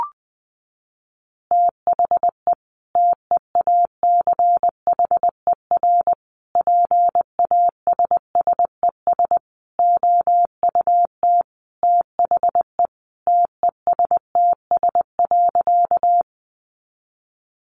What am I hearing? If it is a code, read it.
Morse code, heard as a keyed tone, about 20 wpm, 700 Hz. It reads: THE TEACHER PASSES OUT THE TESTS.
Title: Additional Easy Phrases 20wpm